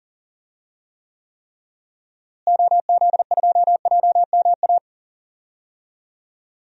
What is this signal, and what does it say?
Morse code, heard as a keyed tone, about 38 wpm, 700 Hz. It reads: OZ2JMA
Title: DXCC-100 Call Signs - 6 Characters - Repeated 3x - 40wpm